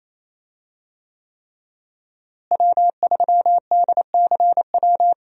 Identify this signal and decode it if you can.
Morse code, heard as a keyed tone, about 28 wpm, 700 Hz. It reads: W3DCW
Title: USA Call Signs - 5 Characters - Repeated 3x - 28wpm